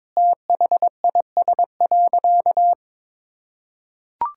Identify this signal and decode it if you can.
Morse code, heard as a keyed tone, about 22 wpm, 700 Hz. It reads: THIS.
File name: practice / 22wpm / Sentences with 8-Word Phrases - 22wpm